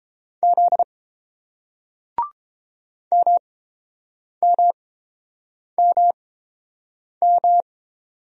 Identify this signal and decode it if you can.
Morse code, heard as a keyed tone, about 30 wpm, spaced as for 7 wpm, 700 Hz. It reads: Z MMMM